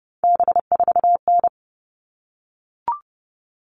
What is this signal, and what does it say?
Morse code, heard as a keyed tone, about 30 wpm, 700 Hz. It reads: B4N
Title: All Cut Numbers - 3-Digit 30wpm